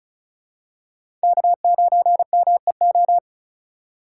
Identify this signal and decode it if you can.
Morse code, heard as a keyed tone, about 36 wpm, 700 Hz. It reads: K9MEO
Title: USA Call Signs - 4x Character Spacing - 35wpm